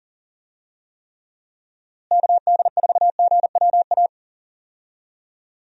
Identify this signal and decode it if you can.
Morse code, heard as a keyed tone, about 37 wpm, 700 Hz. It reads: KD4GWA
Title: USA Call Signs - 6 Characters - Repeated 2x - 40wpm